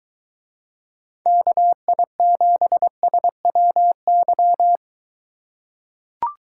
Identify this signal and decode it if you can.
Morse code, heard as a keyed tone, about 23 wpm, 700 Hz. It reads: KI7SWY